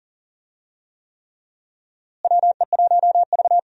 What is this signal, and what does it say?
Morse code, heard as a keyed tone, about 40 wpm, 700 Hz. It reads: WE1V
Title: USA Call Signs - 4 Characters - Repeated 3x - 40wpm